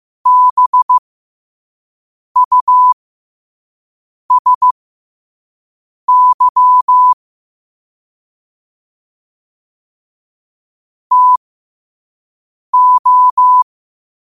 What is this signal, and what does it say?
Morse code, heard as a keyed tone, about 15 wpm, spaced as for 5 wpm, 1000 Hz. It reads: BUSY TO
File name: qotdfiles / 5wpm